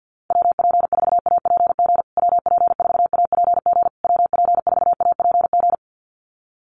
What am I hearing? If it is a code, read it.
Morse code, heard as a keyed tone, about 50 wpm, 700 Hz. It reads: WP4APG WP4APG WP4APG
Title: USA Call Signs Repeated 3x 50wpm